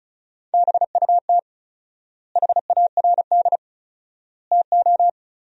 Morse code, heard as a keyed tone, about 34 wpm, 700 Hz. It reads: BUT HARD TO
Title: 4x Word-Spacing - Sentences from Top 300 Words - 35wpm